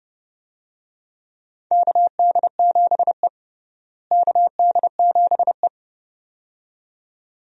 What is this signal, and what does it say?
Morse code, heard as a keyed tone, about 30 wpm, 700 Hz. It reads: KD7E KD7E
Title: USA Call Signs - 4 Characters - Repeated 2x - 30wpm